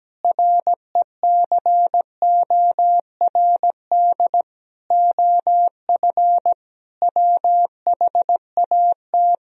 Morse code, heard as a keyed tone, about 17 wpm, 700 Hz. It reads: RECORD OF WHAT
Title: Sentences from Top 1000 Words - Review Part 3 - 17wpm